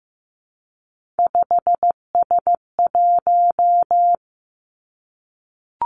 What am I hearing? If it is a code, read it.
Morse code, heard as a keyed tone, about 15 wpm, 700 Hz. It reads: 5S1
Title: Three Challenging Letter-Number Combinations 15wpm